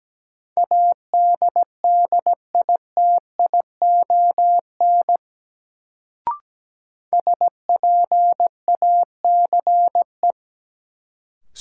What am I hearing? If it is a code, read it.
Morse code, heard as a keyed tone, about 17 wpm, 700 Hz. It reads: ADDITION SPACE
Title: Sentences from Top 1500 Words - New Word Drill Alt - 17wpm